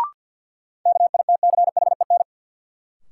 Morse code, heard as a keyed tone, about 53 wpm, 700 Hz. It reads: KITCHEN
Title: Sentences from Top 1400 Words - New Word Drill Alt - 50wpm